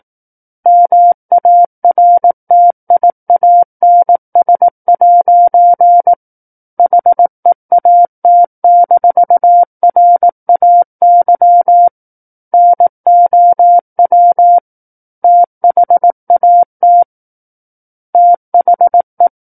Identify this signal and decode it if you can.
Morse code, heard as a keyed tone, about 18 wpm, 700 Hz. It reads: MARTIANS' HEAT-RAY NOW THAT THE
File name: 18 WPM